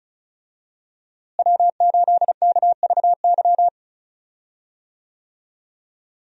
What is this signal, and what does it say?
Morse code, heard as a keyed tone, about 34 wpm, 700 Hz. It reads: W8KVY
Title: USA Call Signs - 5 Characters - Repeated 2x - 35wpm